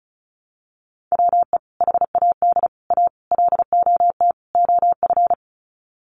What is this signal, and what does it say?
Morse code, heard as a keyed tone, about 34 wpm, 700 Hz. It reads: WE HAD A LOT OF
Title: Sets of 5 Words Encore 35wpm